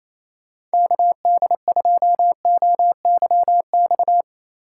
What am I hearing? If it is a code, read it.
Morse code, heard as a keyed tone, about 35 wpm, 700 Hz. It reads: KD2OYX
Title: Speed Racing - USA Call Signs 28wpm